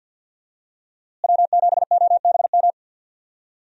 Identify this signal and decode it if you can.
Morse code, heard as a keyed tone, about 53 wpm, 700 Hz. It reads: W7OBM
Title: USA Call Signs - 5 Characters - Repeated 2x - 50wpm